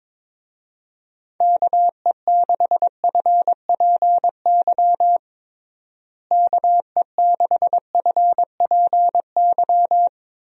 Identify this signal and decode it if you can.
Morse code, heard as a keyed tone, about 22 wpm, 700 Hz. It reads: KE6FPY KE6FPY
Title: USA Call Signs - 6 Characters - Repeated 3x - 22wpm